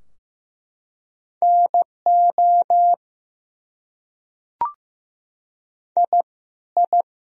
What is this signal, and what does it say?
Morse code, heard as a keyed tone, about 15 wpm, 700 Hz. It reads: NO I I
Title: Top 100 Words Repeated 15wpm